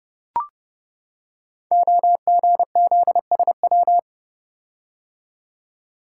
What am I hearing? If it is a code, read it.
Morse code, heard as a keyed tone, about 30 wpm, 700 Hz. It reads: OGZSW
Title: Code Groups - 5 Letter:Numbers - 30wpm